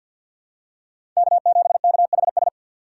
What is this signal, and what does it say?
Morse code, heard as a keyed tone, about 47 wpm, 700 Hz. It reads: K7KHS
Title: USA Call Signs - 5 Characters - 2x Character Spacing - 50wpm